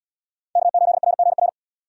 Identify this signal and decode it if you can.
Morse code, heard as a keyed tone, about 75 wpm, 700 Hz. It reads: K8RGR